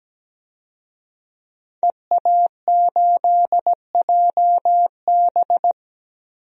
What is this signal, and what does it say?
Morse code, heard as a keyed tone, about 17 wpm, 700 Hz. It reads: EA8JB
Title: DXCC-100 Call Signs - 5 Characters - 17wpm